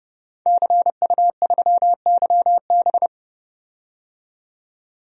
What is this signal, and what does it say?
Morse code, heard as a keyed tone, about 30 wpm, 700 Hz. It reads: CU3YB
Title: DXCC-100 Call Signs - 5 Characters - 30wpm